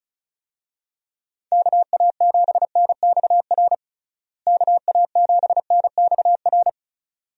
Morse code, heard as a keyed tone, about 36 wpm, 700 Hz. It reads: KA7NXR KA7NXR
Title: USA Call Signs - 6 Characters - Repeated 2x - 35wpm